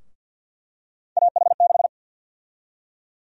Morse code, heard as a keyed tone, about 52 wpm, 700 Hz. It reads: AH6
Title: USA Call Sign Prefixes - Repeated 2x - 50wpm